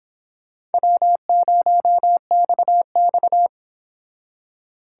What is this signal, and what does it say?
Morse code, heard as a keyed tone, about 29 wpm, 700 Hz. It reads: W0XX